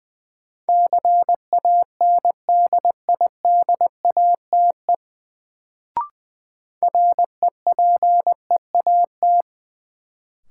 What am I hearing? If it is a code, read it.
Morse code, heard as a keyed tone, about 20 wpm, 700 Hz. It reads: CANDIDATE REPEAT
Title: Sentences from Top 1600 Words - New Word Drill Alt - 20wpm